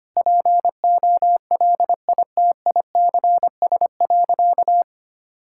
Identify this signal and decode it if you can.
Morse code, heard as a keyed tone, about 25 wpm, 700 Hz. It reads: POLITICS.
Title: Sentences from Top 1000 Words - Review Part 6 - 25wpm